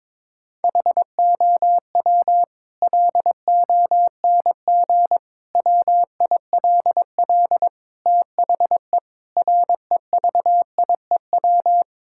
Morse code, heard as a keyed tone, about 22 wpm, 700 Hz. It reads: HOW LONG WILL THE REVIEW
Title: Sentences from Top 1000 Words - Review Part 6 - 22wpm